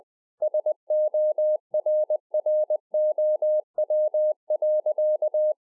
Morse code, heard as a keyed tone, about 20 wpm, 610 Hz. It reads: SORROW.